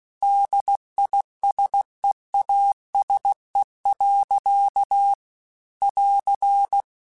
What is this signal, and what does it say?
Morse code, heard as a keyed tone, about 16 wpm, 800 Hz. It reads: DISEASE. <AR>